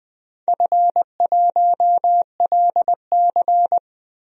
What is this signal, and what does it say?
Morse code, heard as a keyed tone, about 20 wpm, 700 Hz. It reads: F1LC
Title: DXCC-100 Call Signs - 4 Characters - 20wpm